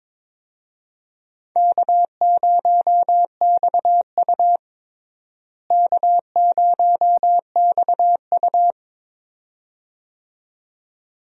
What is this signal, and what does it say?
Morse code, heard as a keyed tone, about 22 wpm, 700 Hz. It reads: K0XU K0XU
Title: USA Call Signs - 4 Characters - Repeated 2x - 22wpm